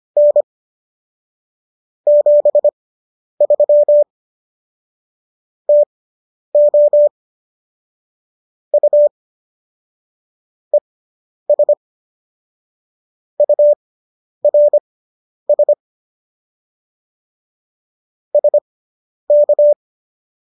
Morse code, heard as a keyed tone, about 25 wpm, spaced as for 10 wpm, 590 Hz. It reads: N 73 TO U ES URS SK